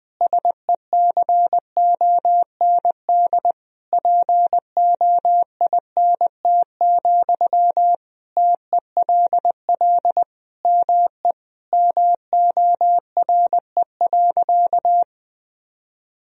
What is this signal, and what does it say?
Morse code, heard as a keyed tone, about 20 wpm, 700 Hz. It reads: SECOND POINT, TELL ME MORE.
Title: Sentences from Top 1000 Words - Review Part 1 - 20wpm